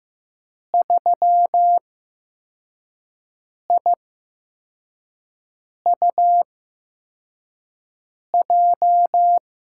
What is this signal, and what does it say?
Morse code, heard as a keyed tone, about 15 wpm, spaced as for 4 wpm, 700 Hz. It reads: 3IUJ
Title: USA Call Signs - 6 Characters - 8x Character Spacing - 15wpm